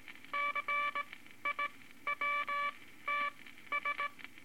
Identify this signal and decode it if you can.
Morse code, heard as a keyed tone, about 19 wpm, spaced as for 13 wpm, 1160 Hz, with noise. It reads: CIWTS
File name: Morse Code Records / 18 B LCM 165 Ten Words Per Minute